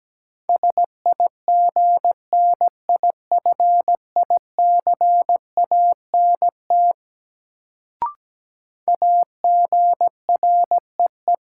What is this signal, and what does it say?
Morse code, heard as a keyed tone, about 17 wpm, 700 Hz. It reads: SIGNIFICANT AGREE
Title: Sentences from Top 1000 Words - Review Part 6 - New Word Drill - 17wpm